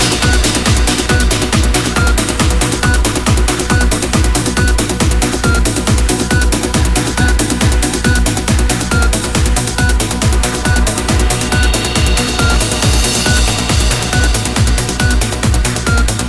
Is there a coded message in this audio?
No.